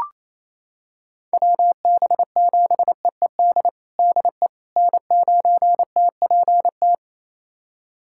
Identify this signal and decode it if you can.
Morse code, heard as a keyed tone, about 28 wpm, 700 Hz. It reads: WB7EED DE N9TPT
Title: USA Callsign DE Callsign 28wpm